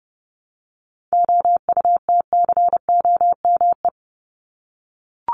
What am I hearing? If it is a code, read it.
Morse code, heard as a keyed tone, about 30 wpm, 700 Hz. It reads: OUTCOME
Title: Top 1501-1750 Words 30wpm